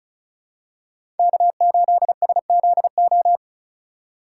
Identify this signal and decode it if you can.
Morse code, heard as a keyed tone, about 36 wpm, 700 Hz. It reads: K8SZO
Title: USA Call Signs - 5 Characters - 4x Character Spacing - 35wpm